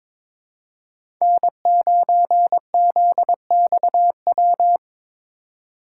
Morse code, heard as a keyed tone, about 22 wpm, 700 Hz. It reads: N9ZXW
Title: USA Call Signs - 5 Characters - 4x Character Spacing - 22wpm